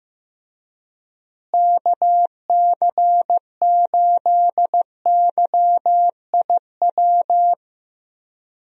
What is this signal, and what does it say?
Morse code, heard as a keyed tone, about 15 wpm, 700 Hz. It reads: KC8YIW
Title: USA Call Signs - 6 Characters - Repeated 3x - 15wpm